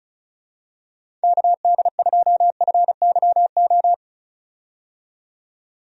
Morse code, heard as a keyed tone, about 34 wpm, 700 Hz. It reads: KD2FYO
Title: USA Call Signs - 6 Characters - 35wpm